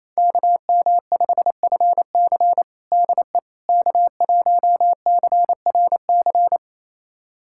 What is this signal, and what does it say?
Morse code, heard as a keyed tone, about 28 wpm, 690 Hz. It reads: KM5FC DE K1CRC
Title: USA Callsign DE Callsign - 5 Characters - 28wpm